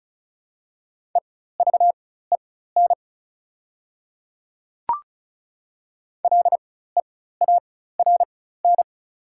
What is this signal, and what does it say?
Morse code, heard as a keyed tone, about 36 wpm, spaced as for 16 wpm, 690 Hz. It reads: EVEN LEARN
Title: 4x Character Spacing - Top 300 Words - 35wpm